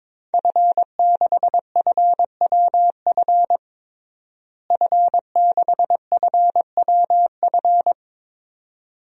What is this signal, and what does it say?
Morse code, heard as a keyed tone, about 21 wpm, 700 Hz. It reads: F6FWF F6FWF
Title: DXCC-100 Call Signs - 5 Characters - Repeated 2x - 22wpm